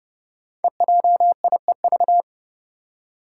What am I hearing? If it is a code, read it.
Morse code, heard as a keyed tone, about 31 wpm, 700 Hz. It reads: EJIEV
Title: Five Challenging Letter-Number Combinations 30wpm